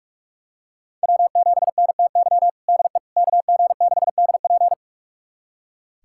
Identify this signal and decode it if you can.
Morse code, heard as a keyed tone, about 42 wpm, 700 Hz. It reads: W7NTY DE KG6DP
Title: USA Callsign DE Callsign - 5 Characters - 45wpm